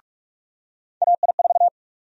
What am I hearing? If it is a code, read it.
Morse code, heard as a keyed tone, about 46 wpm, 700 Hz. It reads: AI4
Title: USA Call Sign Prefixes - 45wpm